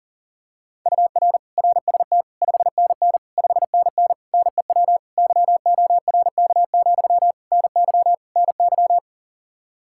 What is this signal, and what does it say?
Morse code, heard as a keyed tone, about 37 wpm, 700 Hz. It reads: UR RST 5NN 5NN NEW YORK, NY NY
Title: Practice QSO - Exchange 1 - Beginning - 40wpm